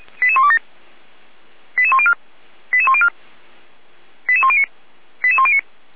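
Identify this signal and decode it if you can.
Five-tone selective call (ZVEI), heard as quick runs of five tones, tones of about 70 ms. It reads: 88127 88104 88104 88109 88109
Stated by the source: ZVEI five-tone selective call received on SDR.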